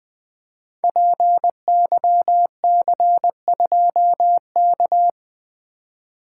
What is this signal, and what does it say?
Morse code, heard as a keyed tone, about 20 wpm, 700 Hz. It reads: PYC2K
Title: Code Groups - 5 Letter:Numbers - 20wpm